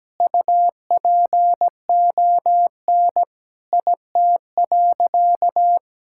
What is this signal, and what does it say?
Morse code, heard as a keyed tone, about 17 wpm, 700 Hz. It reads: UPON IT.
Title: Sentences from Top 1600 Words 17wpm